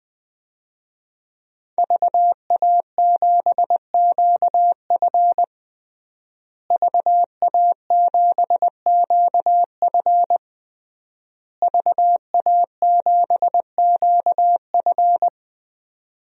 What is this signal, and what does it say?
Morse code, heard as a keyed tone, about 20 wpm, 700 Hz. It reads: VA7QF VA7QF VA7QF
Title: DXCC-100 Call Signs - 5 Characters - Repeated 3x - 20wpm